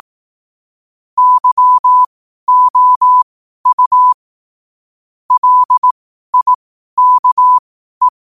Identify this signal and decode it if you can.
Morse code, heard as a keyed tone, about 18 wpm, spaced as for 12 wpm, 1000 Hz. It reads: YOU LIKE
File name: qotdfiles / 12wpm